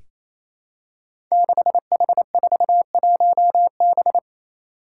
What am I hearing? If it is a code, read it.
Morse code, heard as a keyed tone, about 29 wpm, 700 Hz. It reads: 6H41B
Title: Five Challenging Letter-Number Combinations 28wpm